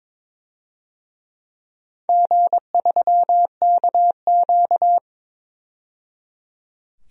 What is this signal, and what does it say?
Morse code, heard as a keyed tone, about 22 wpm, 700 Hz. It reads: G3KQ
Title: DXCC-100 Call Signs - 4 Characters - 22wpm